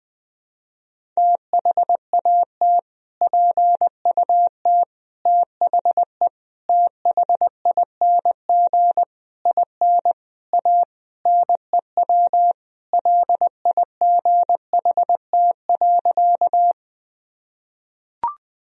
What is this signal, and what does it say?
Morse code, heard as a keyed tone, about 20 wpm, 700 Hz. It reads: THAT PUT THE THING IN A NEW LIGHT.
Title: Sentences from Top 1000 Words - Review Part 1 - 20wpm